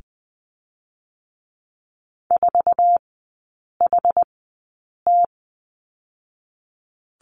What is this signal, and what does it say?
Morse code, heard as a keyed tone, about 19 wpm, spaced as for 8 wpm, 700 Hz. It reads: 4HT
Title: 060 Lesson 11e - Letter h - Three Characters 20wpm